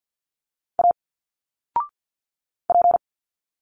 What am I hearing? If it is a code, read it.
Morse code, heard as a keyed tone, about 47 wpm, 700 Hz. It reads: A P
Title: Single Letters 50wpm